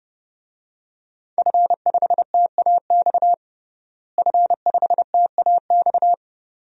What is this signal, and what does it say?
Morse code, heard as a keyed tone, about 32 wpm, 700 Hz. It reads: F5TAX F5TAX
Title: DXCC-100 Call Signs - 5 Characters - Repeated 2x - 30wpm